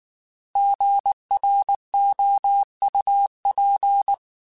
Morse code, heard as a keyed tone, about 19 wpm, 800 Hz. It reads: GROUP